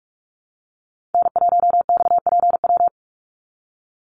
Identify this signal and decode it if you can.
Morse code, heard as a keyed tone, about 44 wpm, 700 Hz. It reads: N1XPW
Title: USA Call Signs 45wpm